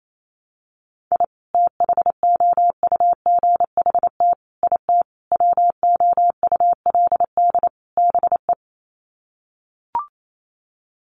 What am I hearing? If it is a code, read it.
Morse code, heard as a keyed tone, about 28 wpm, 700 Hz. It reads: I THOUGHT IT WOULD BE
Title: Sets of 5 Words Encore 28wpm